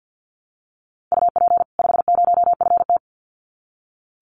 Morse code, heard as a keyed tone, about 48 wpm, 700 Hz. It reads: UP 50FT
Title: Common QSO Phrases 50wpm